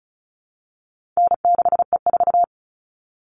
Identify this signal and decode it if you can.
Morse code, heard as a keyed tone, about 34 wpm, 700 Hz. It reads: N6E4
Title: All Cut Numbers - 4-Digit 35wpm